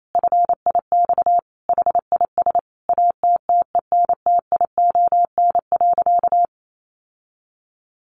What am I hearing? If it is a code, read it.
Morse code, heard as a keyed tone, about 28 wpm, 700 Hz. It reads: FIX HIS ATTENTION.
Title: Sentences from Top 1400 Words 28wpm